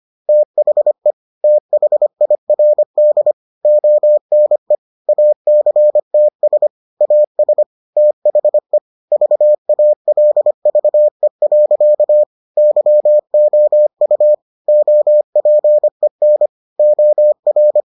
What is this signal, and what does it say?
Morse code, heard as a keyed tone, about 25 wpm, 600 Hz. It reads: THE THIRD ONE ACTS AS THE VALVE. YOU OPEN OR